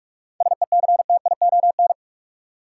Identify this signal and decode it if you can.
Morse code, heard as a keyed tone, about 46 wpm, 700 Hz. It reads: SECTION
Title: Sentences from Top 1000 Words - Review Part 5 - New Word Drill - 45wpm